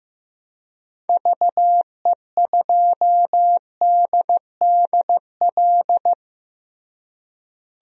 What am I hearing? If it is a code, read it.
Morse code, heard as a keyed tone, about 15 wpm, 700 Hz. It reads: VE2DDL
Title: DXCC-100 Call Signs - 6 Characters - Repeated 2x - 15wpm